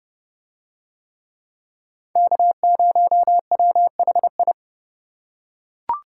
Morse code, heard as a keyed tone, about 30 wpm, 700 Hz. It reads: K0WHI